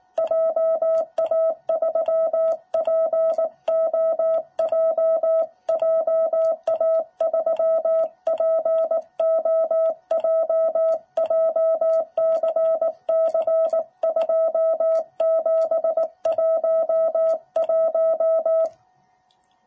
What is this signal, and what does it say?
Morse code, heard as a keyed tone, about 19 wpm, 660 Hz. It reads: JA3POJ JA3POJ JCC2711
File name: jh8cbh / 2021cw